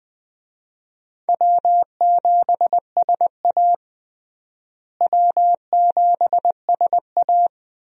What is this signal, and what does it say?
Morse code, heard as a keyed tone, about 21 wpm, 700 Hz. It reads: W7SA W7SA
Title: USA Call Signs - 4 Characters - Repeated 2x - 20wpm